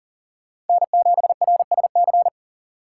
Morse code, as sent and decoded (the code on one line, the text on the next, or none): -. --... .-. ... -.-.
N7RSC